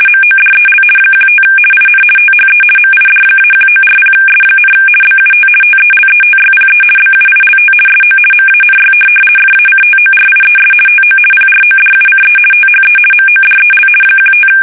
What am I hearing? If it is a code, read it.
RTTY, about 50 baud. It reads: MODEL 15 IN THE MID-1940S, BUT TELETYPE BUILT SO MANY FACTORIES TO PRODUCE THE MODEL 15
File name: fsk850-50B / teletype-corp